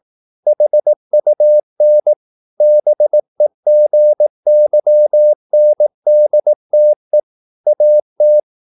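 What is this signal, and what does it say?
Morse code, heard as a keyed tone, about 18 wpm, 600 Hz. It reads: HUN BEGYNDTE AT
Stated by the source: Slow (QRS) Morse practice at 18 wpm, CW med Gnister.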